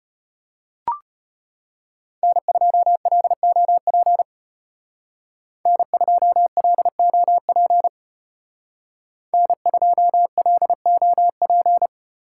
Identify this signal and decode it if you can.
Morse code, heard as a keyed tone, about 32 wpm, 700 Hz. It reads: N2LOP N2LOP N2LOP